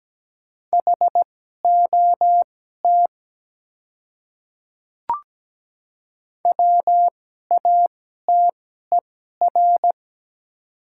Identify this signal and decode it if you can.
Morse code, heard as a keyed tone, about 17 wpm, spaced as for 12 wpm, 700 Hz. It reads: HOT WATER
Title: 2x Character Spacing - Top 100 Words - 17wpm